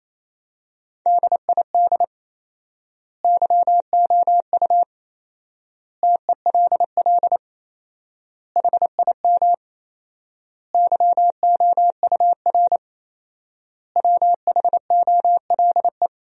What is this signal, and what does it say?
Morse code, heard as a keyed tone, about 28 wpm, 700 Hz. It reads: DID YOU TELL HIM YOUR WHOLE
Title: 4x Word-Spacing - Sentences from Top 400 Words - 28wpm